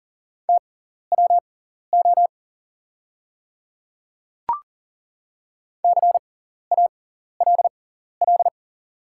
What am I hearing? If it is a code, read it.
Morse code, heard as a keyed tone, about 40 wpm, spaced as for 14 wpm, 700 Hz. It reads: TWO CALL
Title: 6x Character Spacing - Top 200 Words - 40wpm